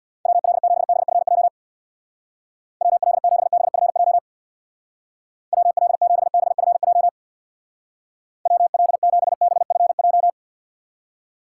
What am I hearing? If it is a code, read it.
Morse code, heard as a keyed tone, about 62 wpm, 700 Hz. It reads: WL7BFJ WL7BFJ WL7BFJ WL7BFJ
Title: Speed Racing - USA 2x3 Call Signs 50wpm